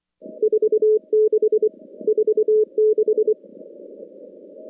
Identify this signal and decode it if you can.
Morse code, heard as a keyed tone, about 23 wpm, 430 Hz. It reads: 46 46
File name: contests / CW-SS-k5tr- -2200